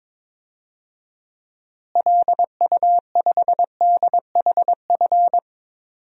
Morse code, heard as a keyed tone, about 21 wpm, 700 Hz. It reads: LU5DHF
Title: DXCC-100 Call Signs - 6 Characters - Repeated 3x - 22wpm